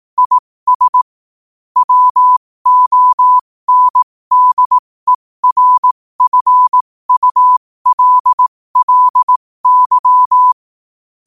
Morse code, heard as a keyed tone, about 18 wpm, spaced as for 15 wpm, 1000 Hz. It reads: IS WONDERFULLY
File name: qotdfiles / 15wpm